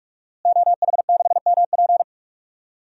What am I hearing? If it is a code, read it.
Morse code, heard as a keyed tone, about 46 wpm, 700 Hz. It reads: OH6MP
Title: DXCC-100 Call Signs - 5 Characters - Repeated 3x - 45wpm